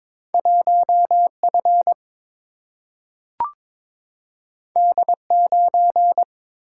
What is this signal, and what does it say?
Morse code, heard as a keyed tone, about 22 wpm, 700 Hz. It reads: 1 F D 9
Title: Single-Letter Single-Number 22wpm